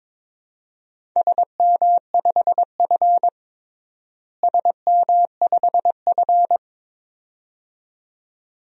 Morse code, heard as a keyed tone, about 22 wpm, 700 Hz. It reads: SM5F SM5F
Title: DXCC-100 Call Signs - 4 Characters - Repeated 2x - 22wpm